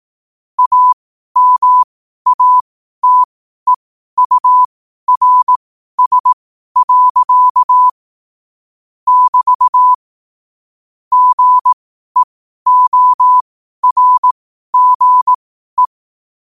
Morse code, heard as a keyed tone, about 18 wpm, spaced as for 12 wpm, 1000 Hz. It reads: AMATEURS. <BT> GEORGE
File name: qotdfiles / 12wpm